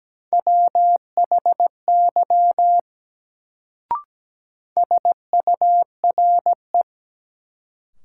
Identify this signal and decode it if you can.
Morse code, heard as a keyed tone, about 17 wpm, 700 Hz. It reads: WHY SURE
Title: Sets of 3 Words - New Word Drill - 17wpm